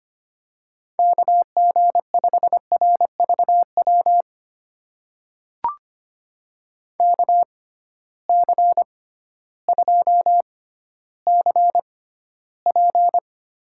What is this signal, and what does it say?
Morse code, heard as a keyed tone, about 25 wpm, spaced as for 14 wpm, 700 Hz. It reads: KG5RVW KC2CP
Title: USA Call Signs - 6 Characters - 6x Character Spacing - 25wpm